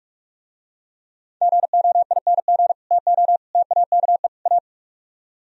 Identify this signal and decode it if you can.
Morse code, heard as a keyed tone, about 46 wpm, 700 Hz. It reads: GOING TO TAKE A